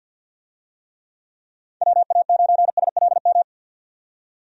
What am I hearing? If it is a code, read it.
Morse code, heard as a keyed tone, about 47 wpm, 700 Hz. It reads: WA9SLM
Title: USA Call Signs - 6 Characters - 50wpm